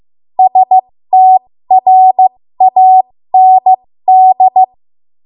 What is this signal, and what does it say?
Morse code, heard as a keyed tone, about 15 wpm, spaced as for 13 wpm, 750 Hz. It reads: STRAND
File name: Morse / 13WPM